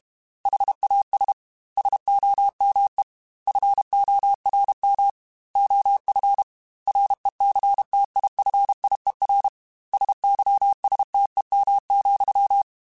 Morse code, heard as a keyed tone, about 32 wpm, 800 Hz. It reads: HAS SOME FORM OF RECTIFIER SYSTEM,
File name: Technical Subjects / Technical Subjects-mp3s / Sw Pwr Sup @30